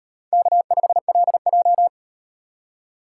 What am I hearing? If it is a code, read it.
Morse code, heard as a keyed tone, about 38 wpm, 700 Hz. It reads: K5LJ